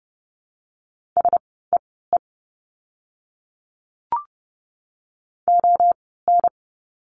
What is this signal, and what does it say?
Morse code, heard as a keyed tone, about 30 wpm, spaced as for 17 wpm, 700 Hz. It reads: SEE ON
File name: course / 30:10wpm / 237 Lesson 38f - Letter q - Words 30wpm